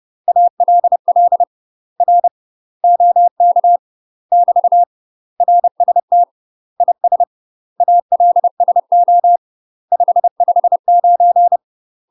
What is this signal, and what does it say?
Morse code, heard as a keyed tone, about 30 wpm, 700 Hz. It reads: ALL R OK <BT> RST IS ALSO 559
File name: UK MASTER QSOs TRAINING / 30WPM / UK MASTER QSOs 0029